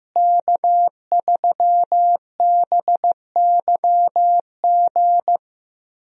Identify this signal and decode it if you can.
Morse code, heard as a keyed tone, about 15 wpm, 700 Hz. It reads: K3BYG
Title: USA Call Signs - 5 Characters - Repeated 3x - 15wpm